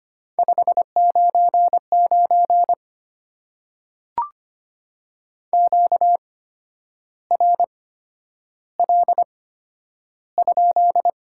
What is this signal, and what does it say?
Morse code, heard as a keyed tone, about 25 wpm, spaced as for 10 wpm, 700 Hz. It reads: 599 QRL?
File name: Top 90 QSO Elements - 8x Character Spacing - 25wpm